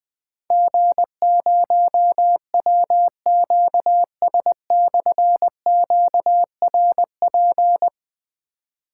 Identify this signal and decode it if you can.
Morse code, heard as a keyed tone, about 20 wpm, 700 Hz. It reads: G0WQS/QRP
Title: Difficult Call Signs 20wpm